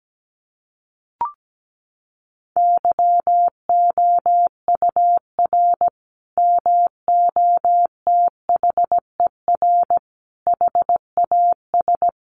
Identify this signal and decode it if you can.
Morse code, heard as a keyed tone, about 17 wpm, 700 Hz. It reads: YOUR MOTHER HAS
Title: Sentences from Top 1200 Words 17wpm